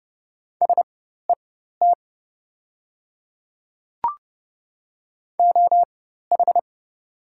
Morse code, heard as a keyed tone, about 29 wpm, spaced as for 14 wpm, 700 Hz. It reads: SET OH
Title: 4x Character Spacing - Top 500 Words - 30wpm